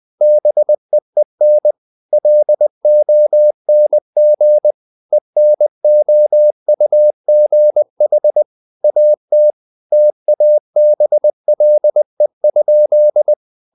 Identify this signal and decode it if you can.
Morse code, heard as a keyed tone, about 20 wpm, 600 Hz. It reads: BEEN LONG ENOUGH AT TABLE?
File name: the old man and the sea / the old man and the sea part 1 20wpm